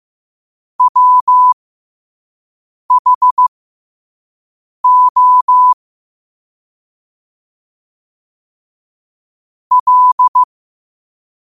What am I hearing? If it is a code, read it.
Morse code, heard as a keyed tone, about 15 wpm, spaced as for 5 wpm, 1000 Hz. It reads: WHO L
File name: qotdfiles / 5wpm